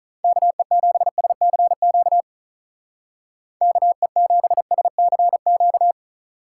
Morse code, heard as a keyed tone, about 43 wpm, 700 Hz. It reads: KE7SCQ KE7SCQ
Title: Speed Racing - USA 2x3 Call Signs 35wpm